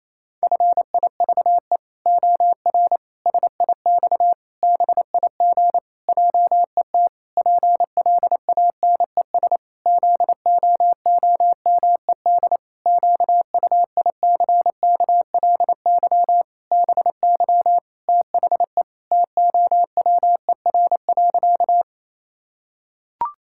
Morse code, heard as a keyed tone, about 28 wpm, 700 Hz. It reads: FIVE OR SIX BIG JET PLANES ZOOMED QUICKLY BY THE TOWER.
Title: Pangram Challenge v2 - 28wpm